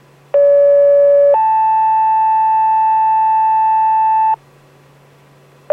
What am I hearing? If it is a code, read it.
Two-tone page, heard as a one-second tone, then a three-second tone, about 570 then 880 Hz, with noise. It is heard from 0.3 to 4.3 s.